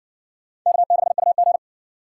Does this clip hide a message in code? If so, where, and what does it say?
Yes: Morse code from 0.7 to 1.6 s, heard as a keyed tone, about 60 wpm, 700 Hz. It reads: K6UG